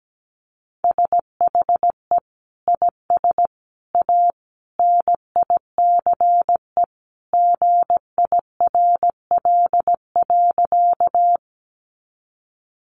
Morse code, heard as a keyed tone, about 17 wpm, 700 Hz. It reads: SHE IS A NICE GIRL.
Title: Sentences from Top 1100 Words 17wpm